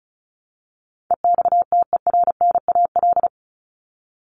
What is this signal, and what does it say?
Morse code, heard as a keyed tone, about 34 wpm, 700 Hz. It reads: EXTERNAL
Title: Top 1751-2000 Words 35wpm